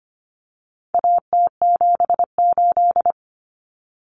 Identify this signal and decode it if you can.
Morse code, heard as a keyed tone, about 25 wpm, 700 Hz. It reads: AT78
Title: Common Cut Numbers - 4-Digit 25wpm